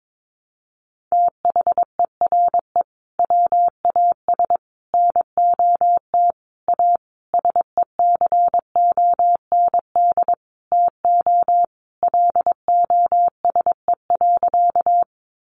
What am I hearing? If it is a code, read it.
Morse code, heard as a keyed tone, about 22 wpm, 700 Hz. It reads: THERE WAS NOT A SECOND TO LOSE.
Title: Sentences from Top 1100 Words 22wpm